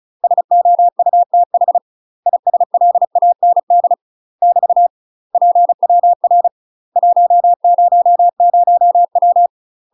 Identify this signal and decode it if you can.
Morse code, heard as a keyed tone, about 36 wpm, 700 Hz. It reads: SOUTH ISLAND <BT> PWR 100W
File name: UK MASTER QSOs TRAINING / 35WPM / UK MASTER QSOs 0017